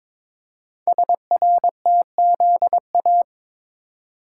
Morse code, heard as a keyed tone, about 22 wpm, 700 Hz. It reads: SRTZA
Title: Code Groups - 5 Letters - 22wpm